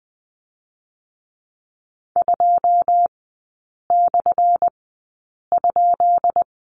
Morse code, heard as a keyed tone, about 20 wpm, spaced as for 8 wpm, 700 Hz. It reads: 2/?
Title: 178 Lesson 29e - Slash - Three Characters 20wpm